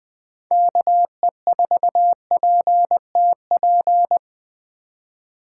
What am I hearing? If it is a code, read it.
Morse code, heard as a keyed tone, about 21 wpm, 700 Hz. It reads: KE4PTP
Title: 2023 POTA Hunter Call Signs - 20wpm